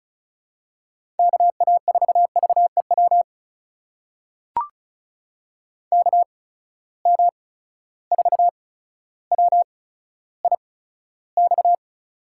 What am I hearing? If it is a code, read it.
Morse code, heard as a keyed tone, about 36 wpm, spaced as for 15 wpm, 700 Hz. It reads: KA4VEW KM4WIX
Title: USA Call Signs - 6 Characters - 8x Character Spacing - 35wpm